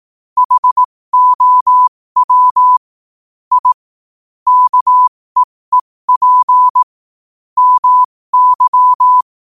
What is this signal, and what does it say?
Morse code, heard as a keyed tone, about 18 wpm, spaced as for 15 wpm, 1000 Hz. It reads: HOW I KEEP MY
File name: qotdfiles / 15wpm